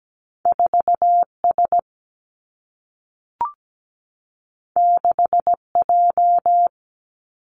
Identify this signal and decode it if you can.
Morse code, heard as a keyed tone, about 17 wpm, 700 Hz. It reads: 4 S 6 J
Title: Two Challenging Letter-Number Combinations 17wpm